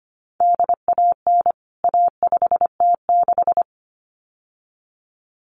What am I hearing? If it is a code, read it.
Morse code, heard as a keyed tone, about 25 wpm, 700 Hz. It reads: DAN A5T6
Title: CWT Exchange 25wpm